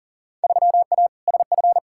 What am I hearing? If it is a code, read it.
Morse code, heard as a keyed tone, about 40 wpm, 700 Hz. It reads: 3A SF
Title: Field Day Exchanges Encore - 40wpm